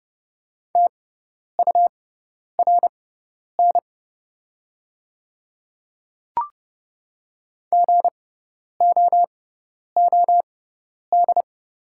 Morse code, heard as a keyed tone, about 30 wpm, spaced as for 10 wpm, 690 Hz. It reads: TURN GOOD
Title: 6x Character Spacing - Top 200 Words - 30wpm